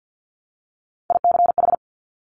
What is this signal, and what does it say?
Morse code, heard as a keyed tone, about 50 wpm, 700 Hz. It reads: ICH
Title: Top Three Letter Combinations in Words 50wpm